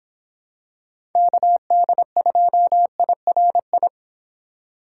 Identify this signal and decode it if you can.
Morse code, heard as a keyed tone, about 26 wpm, 700 Hz. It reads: KD2IRI